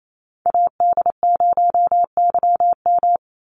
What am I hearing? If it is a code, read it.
Morse code, heard as a keyed tone, about 28 wpm, 700 Hz. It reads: AD0YM
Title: NA SOTA Chaser Callsigns 28wpm